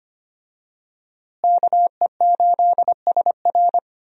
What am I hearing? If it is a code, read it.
Morse code, heard as a keyed tone, about 25 wpm, 700 Hz. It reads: KE8SR
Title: USA Call Signs - 5 Characters - Repeated 3x - 25wpm